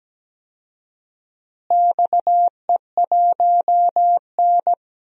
Morse code, heard as a keyed tone, about 17 wpm, 700 Hz. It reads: XE1N
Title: DXCC-100 Call Signs - 4 Characters - Repeated 2x - 17wpm